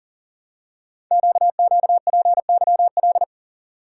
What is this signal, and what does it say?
Morse code, heard as a keyed tone, about 40 wpm, 700 Hz. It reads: QQPYL
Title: Code Groups - 5 Letters - 40wpm